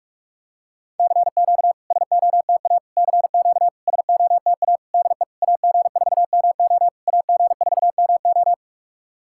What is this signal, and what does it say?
Morse code, heard as a keyed tone, about 44 wpm, 690 Hz. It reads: CQ SOTA CQ SOTA DE AG4MO AG4MO
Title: SOTA QSO Practice 45wpm